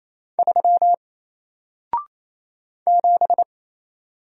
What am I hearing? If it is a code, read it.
Morse code, heard as a keyed tone, about 27 wpm, 700 Hz. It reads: 3 7
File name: Single-Letter-Number-28wpm